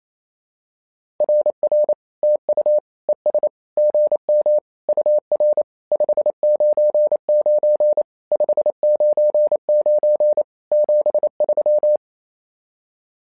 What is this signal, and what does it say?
Morse code, heard as a keyed tone, about 29 wpm, 600 Hz. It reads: RR TU ES GM UR 599 599 73
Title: SOTA QSO Practice 28wpm